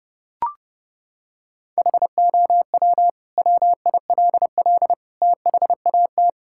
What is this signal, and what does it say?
Morse code, heard as a keyed tone, about 31 wpm, 700 Hz. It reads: HOW WILL THAT
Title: Sentences from Top 1000 Words - Review Part 6 - 30wpm